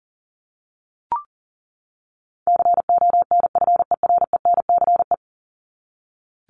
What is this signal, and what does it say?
Morse code, heard as a keyed tone, about 40 wpm, 700 Hz. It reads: CONFERENCE
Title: Top 1751-2000 Words 40wpm